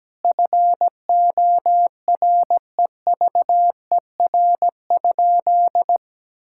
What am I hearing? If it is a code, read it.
Morse code, heard as a keyed tone, about 17 wpm, 700 Hz. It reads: FOREVER?
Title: Sentences from Top 1600 Words 17wpm